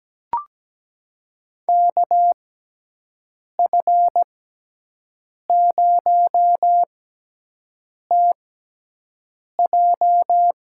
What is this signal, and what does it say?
Morse code, heard as a keyed tone, about 17 wpm, spaced as for 6 wpm, 700 Hz. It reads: KF0TJ
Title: USA Call Signs - 5 Characters - 6x Character Spacing - 17wpm